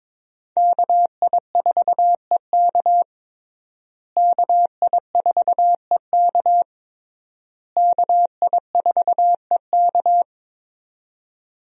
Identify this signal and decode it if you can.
Morse code, heard as a keyed tone, about 21 wpm, 700 Hz. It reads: KI4EK KI4EK KI4EK
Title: USA Call Signs - 5 Characters - Repeated 3x - 22wpm